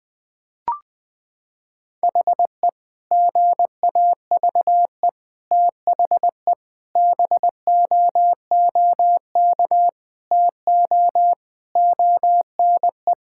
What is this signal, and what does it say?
Morse code, heard as a keyed tone, about 20 wpm, 700 Hz. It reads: HE GAVE THE BOOK TO ONE
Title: Sentences with 4-Word Phrases - 20wpm